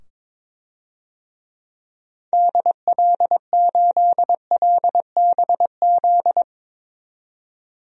Morse code, heard as a keyed tone, about 22 wpm, 700 Hz. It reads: DL8LBZ
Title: DXCC-100 Call Signs - 6 Characters - 22wpm